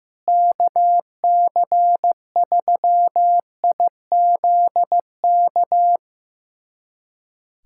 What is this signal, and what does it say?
Morse code, heard as a keyed tone, about 15 wpm, 700 Hz. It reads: KC3IZK
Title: USA Callsign DE Callsign 15wpm